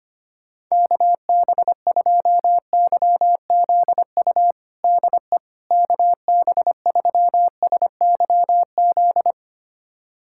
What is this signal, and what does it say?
Morse code, heard as a keyed tone, about 25 wpm, 700 Hz. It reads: KB2YZU DE KB3SYZ
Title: USA Callsign DE Callsign 25wpm